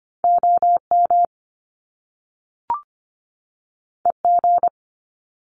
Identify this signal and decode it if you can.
Morse code, heard as a keyed tone, about 25 wpm, 700 Hz. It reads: OM EG
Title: Top Two Letter Combinations in Words 25wpm